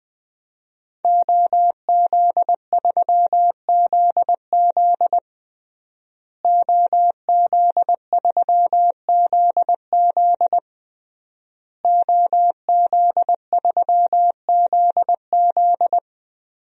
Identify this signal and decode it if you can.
Morse code, heard as a keyed tone, about 20 wpm, 700 Hz. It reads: OZ3ZZ OZ3ZZ OZ3ZZ
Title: DXCC-100 Call Signs - 5 Characters - Repeated 3x - 20wpm